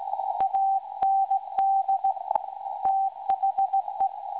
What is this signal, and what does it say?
Morse code, heard as a keyed tone, about 17 wpm, 780 Hz. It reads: AND THE
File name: WebSDR records / SAQ recording 17.2kHz